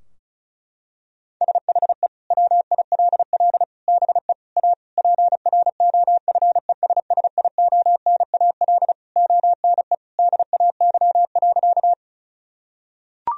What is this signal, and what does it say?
Morse code, heard as a keyed tone, about 36 wpm, 700 Hz. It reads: SHE WILL BE A PROFESSIONAL ONE DAY.
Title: Sentences from Top 1000 Words - Review Part 4 - 35wpm